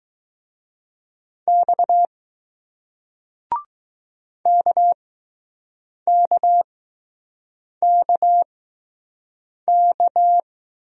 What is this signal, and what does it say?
Morse code, heard as a keyed tone, about 20 wpm, spaced as for 6 wpm, 700 Hz. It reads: X KKKK